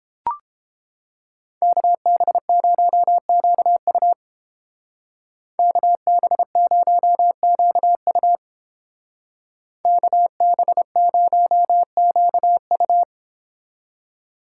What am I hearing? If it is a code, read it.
Morse code, heard as a keyed tone, about 29 wpm, 700 Hz. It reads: KB0QU KB0QU KB0QU